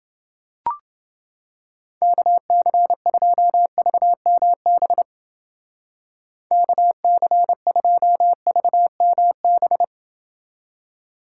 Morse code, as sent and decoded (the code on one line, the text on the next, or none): -.- -.-. ..--- ...- -- -... / -.- -.-. ..--- ...- -- -...
KC2VMB KC2VMB